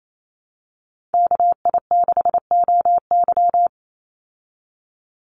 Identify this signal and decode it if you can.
Morse code, heard as a keyed tone, about 28 wpm, 700 Hz. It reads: KI6OY
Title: Call Signs Repeated 3x - 28wpm